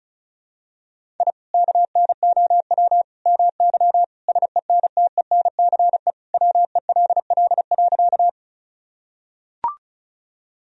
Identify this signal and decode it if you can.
Morse code, heard as a keyed tone, about 36 wpm, 700 Hz. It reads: I KNOW MY SENTENCE WELL.
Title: Sentences from Top 1000 Words - Review Part 1 - 35wpm